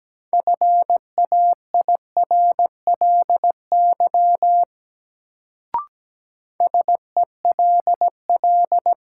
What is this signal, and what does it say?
Morse code, heard as a keyed tone, about 17 wpm, 700 Hz. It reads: FAIRLY SELL
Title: Sentences from Top 1100 Words - New Word Drill Alt - 17wpm